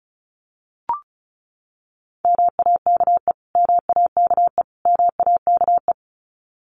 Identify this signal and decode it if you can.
Morse code, heard as a keyed tone, about 34 wpm, 700 Hz. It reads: MAKE MAKE MAKE
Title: Top 200 Words Repeated 35wpm